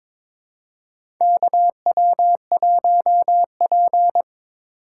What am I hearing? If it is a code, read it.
Morse code, heard as a keyed tone, about 22 wpm, 700 Hz. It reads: KW1P
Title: USA Call Signs - 4 Characters - 3x Character Spacing - 22wpm